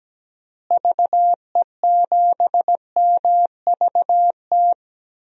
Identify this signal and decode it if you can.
Morse code, heard as a keyed tone, about 17 wpm, 700 Hz. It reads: VE7MVT
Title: DXCC-100 Call Signs - 6 Characters - Repeated 2x - 17wpm